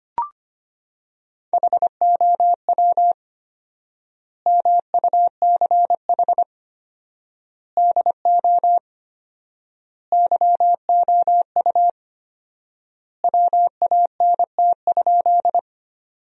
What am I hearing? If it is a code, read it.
Morse code, heard as a keyed tone, about 25 wpm, 700 Hz. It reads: HOW MUCH DO YOU WANT?
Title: 4x Word-Spacing - Sentences from Top 200 Words - 25wpm